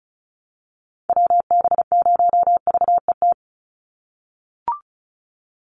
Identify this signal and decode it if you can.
Morse code, heard as a keyed tone, about 34 wpm, 700 Hz. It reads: WB0VET
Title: USA Call Signs 35wpm